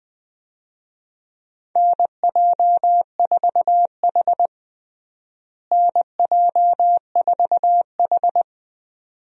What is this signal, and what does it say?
Morse code, heard as a keyed tone, about 20 wpm, 700 Hz. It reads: NJ4H NJ4H
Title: USA Call Signs - 4 Characters - Repeated 2x - 20wpm